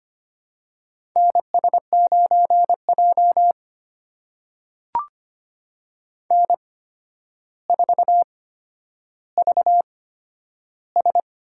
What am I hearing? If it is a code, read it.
Morse code, heard as a keyed tone, about 25 wpm, spaced as for 11 wpm, 700 Hz. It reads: NS9J N4VS
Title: USA Call Signs - 4 Characters - 8x Character Spacing - 25wpm